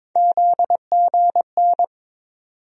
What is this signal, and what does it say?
Morse code, heard as a keyed tone, about 22 wpm, 700 Hz. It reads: ZGN
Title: USA Call Sign Postfixes - 3 Letters 22wpm